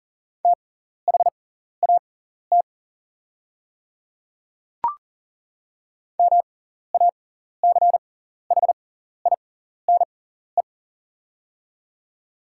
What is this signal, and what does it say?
Morse code, heard as a keyed tone, about 41 wpm, spaced as for 14 wpm, 700 Hz. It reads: THAT MACHINE
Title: 6x Character Spacing - Top 500 Words - 40wpm